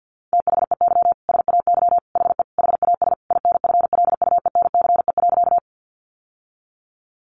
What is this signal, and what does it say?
Morse code, heard as a keyed tone, about 50 wpm, 700 Hz. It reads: THEY SAY HE HAS INFLUENCE.
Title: Sentences from Top 1100 Words 50wpm